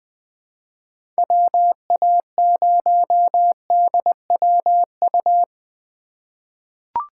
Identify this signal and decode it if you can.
Morse code, heard as a keyed tone, about 20 wpm, 700 Hz. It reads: WA0DWU
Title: USA Call Signs - 6 Characters - 3x Character Spacing - 20wpm